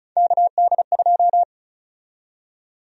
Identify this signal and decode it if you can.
Morse code, heard as a keyed tone, about 36 wpm, 700 Hz. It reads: KD2
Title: USA Call Sign Prefixes - 35wpm